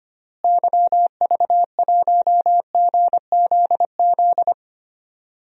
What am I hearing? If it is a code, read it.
Morse code, heard as a keyed tone, about 25 wpm, 700 Hz. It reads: YV1GZZ
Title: DXCC-100 Call Signs - 6 Characters - Repeated 2x - 25wpm